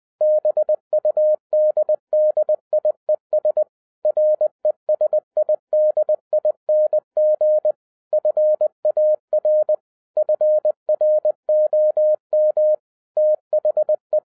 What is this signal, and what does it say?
Morse code, heard as a keyed tone, about 20 wpm, 610 Hz. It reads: BUDDIES RESIDING FAR FROM THE